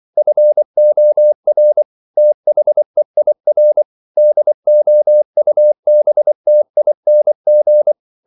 Morse code, heard as a keyed tone, about 24 wpm, 600 Hz. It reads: FOR THEIR DOUBTING
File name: CW / Practice / If24